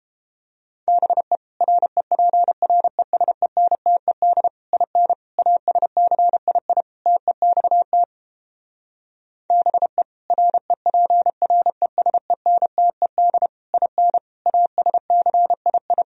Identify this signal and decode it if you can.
Morse code, heard as a keyed tone, about 31 wpm, 700 Hz. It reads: BE REPRESENTED IN ASCII TEXT BE REPRESENTED IN ASCII